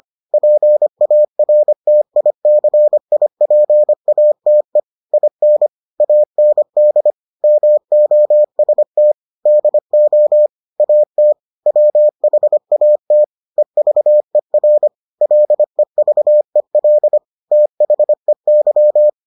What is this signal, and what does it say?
Morse code, heard as a keyed tone, about 25 wpm, 600 Hz. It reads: PARTICIPATE IN AND MOST DO AT WHAT EVER LEVEL THEY